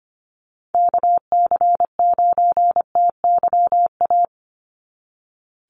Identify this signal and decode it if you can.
Morse code, heard as a keyed tone, about 25 wpm, 700 Hz. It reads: KC9TYA
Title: NA SOTA Chaser Callsigns 25wpm